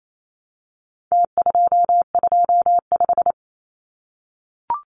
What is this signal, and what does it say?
Morse code, heard as a keyed tone, about 28 wpm, 700 Hz. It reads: T225
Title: Numbers - Contest Style - 4-Digit 28wpm